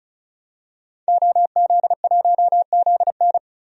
Morse code, heard as a keyed tone, about 34 wpm, 700 Hz. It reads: OZ1ZN
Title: DXCC-100 Call Signs - 5 Characters - 35wpm